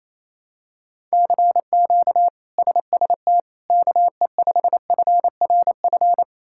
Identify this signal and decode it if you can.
Morse code, heard as a keyed tone, about 28 wpm, 700 Hz. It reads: CQ SST KE5FRF
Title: K1USN SST QSO Practice - 28wpm